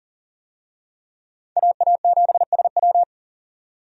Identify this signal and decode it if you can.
Morse code, heard as a keyed tone, about 38 wpm, 700 Hz. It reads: AA7SW
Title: USA Call Signs - 5 Characters - 40wpm